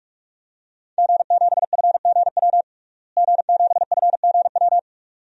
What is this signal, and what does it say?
Morse code, heard as a keyed tone, about 45 wpm, 700 Hz. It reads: G7FGW G7FGW
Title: DXCC-100 Call Signs - 5 Characters - Repeated 2x - 45wpm